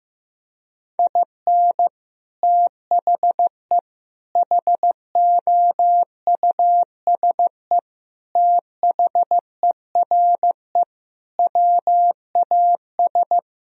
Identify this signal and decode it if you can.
Morse code, heard as a keyed tone, about 15 wpm, 700 Hz. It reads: IN THE HOUSE THERE WAS
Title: Sentences from Top 1000 Words - Review Part 2 - 15wpm